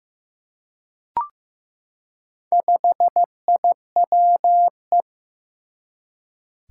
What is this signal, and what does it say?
Morse code, heard as a keyed tone, about 15 wpm, 700 Hz. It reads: 5IWE
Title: Four Challenging Letter-Number Combinations 15wpm